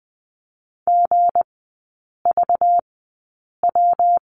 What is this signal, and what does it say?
Morse code, heard as a keyed tone, about 20 wpm, spaced as for 8 wpm, 700 Hz. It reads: GVW
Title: 191 Lesson 31e - Letter v - Three Characters 20wpm